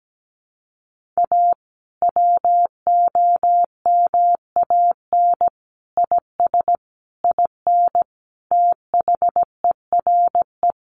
Morse code, heard as a keyed tone, about 17 wpm, 700 Hz. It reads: A WOMAN IS IN THERE
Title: Sentences from Top 1400 Words 17wpm